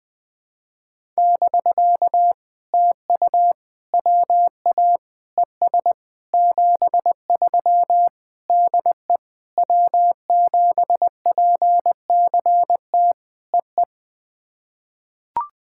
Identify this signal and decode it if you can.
Morse code, heard as a keyed tone, about 20 wpm, 700 Hz. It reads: <BK> TU WA ES 73 DE W7PCT EE
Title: POTA QSO Practice - 20wpm